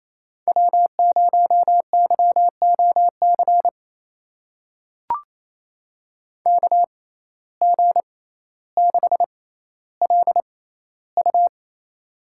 Morse code, heard as a keyed tone, about 27 wpm, spaced as for 14 wpm, 700 Hz. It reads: W0YOC KG6LU
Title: USA Call Signs - 5 Characters - 6x Character Spacing - 28wpm